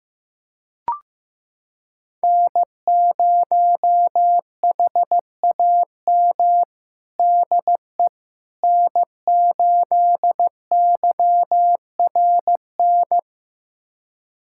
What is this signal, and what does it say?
Morse code, heard as a keyed tone, about 15 wpm, 700 Hz. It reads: N0HAM DE N8YRN
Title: USA Callsign DE Callsign - 5 Characters - 15wpm